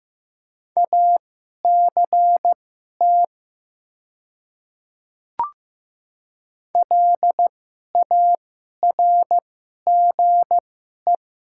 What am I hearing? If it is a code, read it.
Morse code, heard as a keyed tone, about 15 wpm, spaced as for 11 wpm, 700 Hz. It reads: ACT LARGE
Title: 2x Character Spacing - Top 200 Words - 15wpm